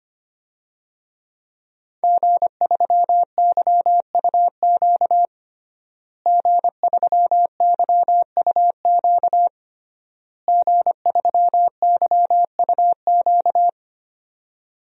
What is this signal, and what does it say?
Morse code, heard as a keyed tone, about 25 wpm, 700 Hz. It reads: G3YUQ G3YUQ G3YUQ
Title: DXCC-100 Call Signs - 5 Characters - Repeated 3x - 25wpm